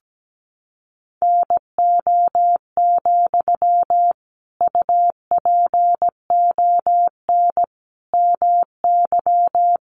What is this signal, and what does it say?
Morse code, heard as a keyed tone, about 17 wpm, 700 Hz. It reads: NO, UPON MY
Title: Sentences from Top 1100 Words 17wpm